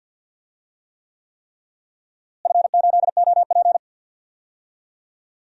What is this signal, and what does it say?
Morse code, heard as a keyed tone, about 49 wpm, 700 Hz. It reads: F8OP
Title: DXCC-100 Call Signs - 4 Characters - Repeated 3x - 50wpm